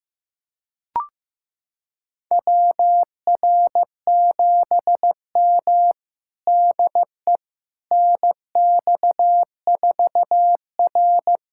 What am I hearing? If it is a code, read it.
Morse code, heard as a keyed tone, about 15 wpm, 700 Hz. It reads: WR7M DE NX4R
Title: USA Callsign DE Callsign - 4 Characters - 15wpm